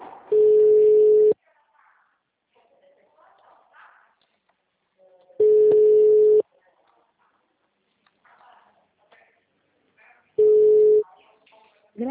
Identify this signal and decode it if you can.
Ringback tone (Europe, 425 Hz), with speech.